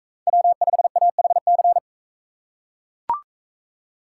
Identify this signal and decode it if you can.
Morse code, heard as a keyed tone, about 40 wpm, 700 Hz. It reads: W5AHC